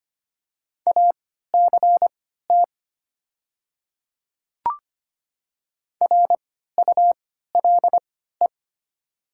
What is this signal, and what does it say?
Morse code, heard as a keyed tone, about 25 wpm, spaced as for 14 wpm, 700 Hz. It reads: ACT RULE
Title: 3x Character Spacing - Top 500 Words - 25wpm